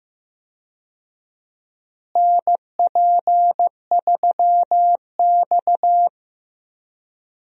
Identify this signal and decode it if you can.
Morse code, heard as a keyed tone, about 15 wpm, 700 Hz. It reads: NP3X
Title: USA Call Signs - 4 Characters - Repeated 2x - 15wpm